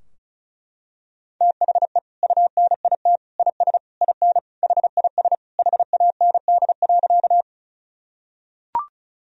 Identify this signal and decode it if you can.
Morse code, heard as a keyed tone, about 36 wpm, 700 Hz. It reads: THE UNIT IS IN HIS HAND.
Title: Sentences from Top 1000 Words - Review Part 2 - 35wpm